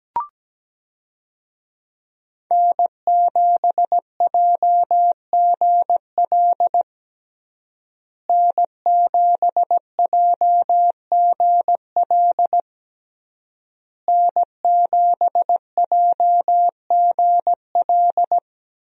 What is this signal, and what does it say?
Morse code, heard as a keyed tone, about 17 wpm, 700 Hz. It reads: N7JGL N7JGL N7JGL
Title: USA Call Signs - 5 Characters - Repeated 3x - 17wpm